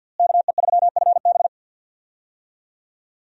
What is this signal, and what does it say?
Morse code, heard as a keyed tone, about 51 wpm, 700 Hz. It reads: KE3FB
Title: USA Call Signs - 5 Characters - Repeated 2x - 50wpm